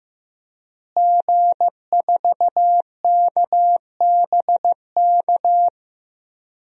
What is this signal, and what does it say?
Morse code, heard as a keyed tone, about 15 wpm, 700 Hz. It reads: G4KBK
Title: DXCC-100 Call Signs - 5 Characters - Repeated 2x - 15wpm